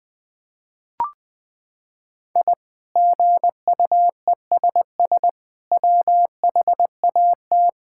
Morse code, heard as a keyed tone, about 20 wpm, 700 Hz. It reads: I GUESS WHAT
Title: Sentences with 8-Word Phrases - 20wpm